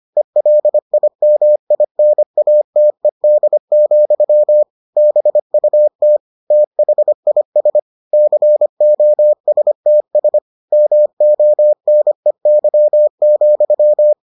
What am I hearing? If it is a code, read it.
Morse code, heard as a keyed tone, about 25 wpm, 600 Hz. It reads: ELIMINATED, BUT THIS COSTS MONEY,